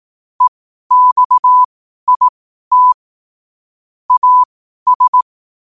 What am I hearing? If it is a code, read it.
Morse code, heard as a keyed tone, about 18 wpm, spaced as for 12 wpm, 1000 Hz. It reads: EXIT AS
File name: qotdfiles / 12wpm